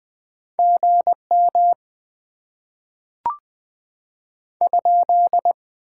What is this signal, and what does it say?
Morse code, heard as a keyed tone, about 20 wpm, 700 Hz. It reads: GM ?
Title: Top 90 QSO Elements - 6x Character Spacing - 20wpm